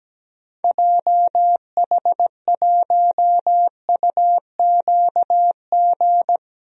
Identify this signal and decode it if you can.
Morse code, heard as a keyed tone, about 17 wpm, 700 Hz. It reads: JH1UQG
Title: DXCC-100 Call Signs - 6 Characters - Repeated 3x - 17wpm